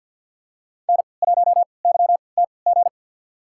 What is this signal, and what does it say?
Morse code, heard as a keyed tone, about 49 wpm, spaced as for 28 wpm, 700 Hz. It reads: N1YTG
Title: USA Call Signs - 5 Characters - 3x Character Spacing - 50wpm